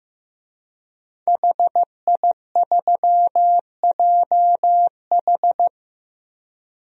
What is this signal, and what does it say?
Morse code, heard as a keyed tone, about 15 wpm, 700 Hz. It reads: HI3JH
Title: DXCC-100 Call Signs - 5 Characters - Repeated 2x - 15wpm